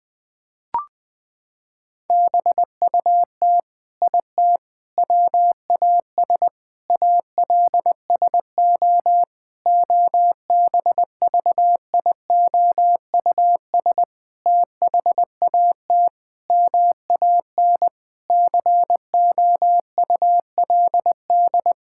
Morse code, heard as a keyed tone, about 19 wpm, 700 Hz. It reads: BUT IT WAS ALSO OBVIOUS THAT MAN COULD
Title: Sentences from Top 1600 Words 20wpm